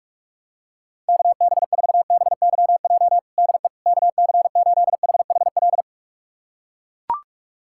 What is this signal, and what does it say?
Morse code, heard as a keyed tone, about 44 wpm, 700 Hz. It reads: KB4BYJ DE KC8HHL
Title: USA Callsign DE Callsign 45wpm